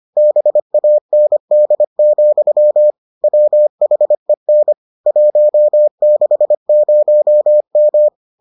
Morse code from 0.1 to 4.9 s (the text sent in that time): BAND, WHEN